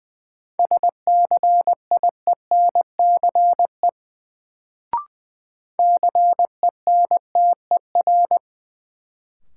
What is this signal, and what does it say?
Morse code, heard as a keyed tone, about 20 wpm, 700 Hz. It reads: SCIENCE CENTER
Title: Sentences from Top 1000 Words - Review Part 2 - New Word Drill - 20wpm